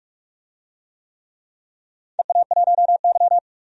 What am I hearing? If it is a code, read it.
Morse code, heard as a keyed tone, about 44 wpm, 700 Hz. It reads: EA1Y
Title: DXCC-100 Call Signs - 4 Characters - 45wpm